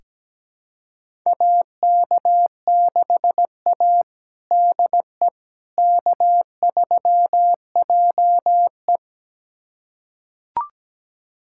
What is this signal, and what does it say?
Morse code, heard as a keyed tone, about 17 wpm, 700 Hz. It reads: AK6A DE K3JE
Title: USA Callsign DE Callsign - 4 Characters - 17wpm